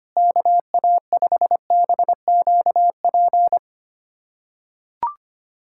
Morse code, heard as a keyed tone, about 25 wpm, 700 Hz. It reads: KA5BQP
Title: USA Callsign DE Callsign - 6 Characters - 25wpm